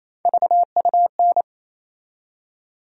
Morse code, heard as a keyed tone, about 27 wpm, 700 Hz. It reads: VUN
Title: USA Call Sign Postfixes - 3 Letters 28wpm